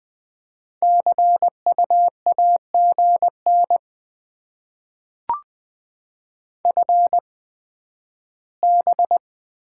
Morse code, heard as a keyed tone, about 20 wpm, spaced as for 12 wpm, 700 Hz. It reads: CU AGN FB
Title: Top 20 QSO Elements - 8x Character Spacing - 20wpm